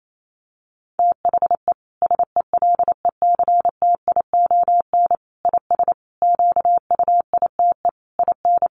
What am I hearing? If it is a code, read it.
Morse code, heard as a keyed tone, about 28 wpm, 700 Hz. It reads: THE SELECTION IS QUITE IN
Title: Sentences from Top 1400 Words 28wpm